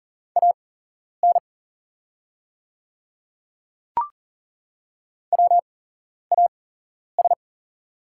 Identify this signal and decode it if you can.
Morse code, heard as a keyed tone, about 40 wpm, spaced as for 11 wpm, 700 Hz. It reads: AN WAS
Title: 8x Character Spacing - Top 100 Words - 40wpm